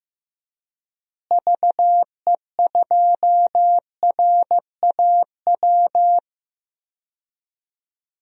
Morse code, heard as a keyed tone, about 15 wpm, 700 Hz. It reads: VE2RAW
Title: DXCC-100 Call Signs - 6 Characters - Repeated 3x - 15wpm